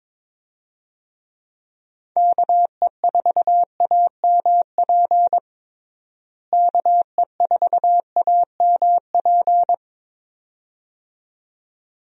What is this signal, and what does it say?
Morse code, heard as a keyed tone, about 22 wpm, 700 Hz. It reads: KE4AMP KE4AMP
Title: USA Call Signs - 6 Characters - Repeated 2x - 22wpm